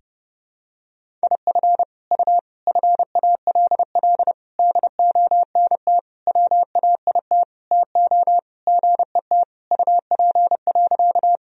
Morse code, heard as a keyed tone, about 32 wpm, 700 Hz. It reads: IF U FALL DONT WAIT TO GET UP.
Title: Sentences from Top 500 Words and Abbreviations 30wpm